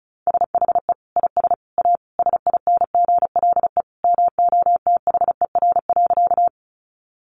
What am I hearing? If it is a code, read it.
Morse code, heard as a keyed tone, about 34 wpm, 700 Hz. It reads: SHE IS A SINGLE MOTHER.
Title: Sentences from Top 700 Words 35wpm